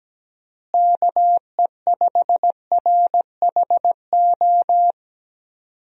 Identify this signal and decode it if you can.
Morse code, heard as a keyed tone, about 17 wpm, 700 Hz. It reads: KE5RHO
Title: USA Call Signs - 6 Characters - Repeated 3x - 17wpm